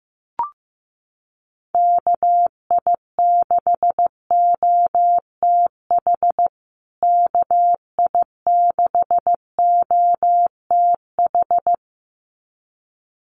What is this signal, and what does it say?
Morse code, heard as a keyed tone, about 15 wpm, 700 Hz. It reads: KI6OTH KI6OTH
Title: USA Call Signs Repeated 2x 15wpm